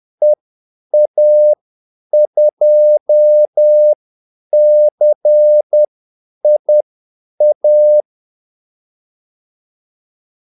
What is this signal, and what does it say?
Morse code, heard as a keyed tone, about 10 wpm, spaced as for 8 wpm, 600 Hz. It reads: EA2CIA
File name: Callsigns-10-8wpm